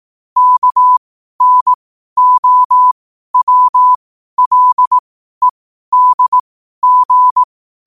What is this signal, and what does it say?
Morse code, heard as a keyed tone, about 18 wpm, spaced as for 12 wpm, 1000 Hz. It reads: KNOWLEDG
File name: qotdfiles / 12wpm